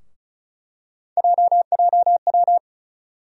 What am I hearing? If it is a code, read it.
Morse code, heard as a keyed tone, about 36 wpm, 700 Hz. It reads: JJW
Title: USA Call Sign Postfixes - 3 Letters 35wpm